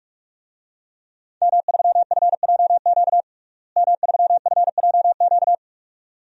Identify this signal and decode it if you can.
Morse code, heard as a keyed tone, about 46 wpm, 700 Hz. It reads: M3FJQ M3FJQ
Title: DXCC-100 Call Signs - 5 Characters - Repeated 2x - 45wpm